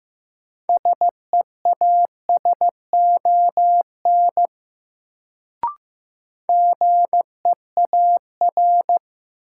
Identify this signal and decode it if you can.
Morse code, heard as a keyed tone, about 15 wpm, 700 Hz. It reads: SEASON GEAR
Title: Sentences from Top 1400 Words - New Word Drill Alt - 15wpm